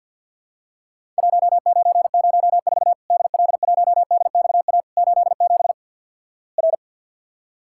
Morse code, heard as a keyed tone, about 50 wpm, 700 Hz. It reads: 1904 DL1DXA 87 R
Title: Worked All Europe DX Contest - QTC Exchange - 50wpm